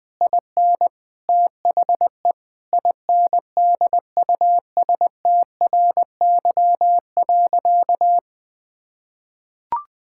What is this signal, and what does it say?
Morse code, heard as a keyed tone, about 21 wpm, 700 Hz. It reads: IN THE INDUSTRY.
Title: Sentences with 8-Word Phrases - 20wpm